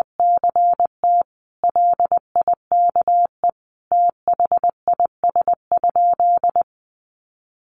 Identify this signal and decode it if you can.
Morse code, heard as a keyed tone, about 20 wpm, 700 Hz. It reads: CT LIKE THIS?
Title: Sentences from Top 1200 Words 20wpm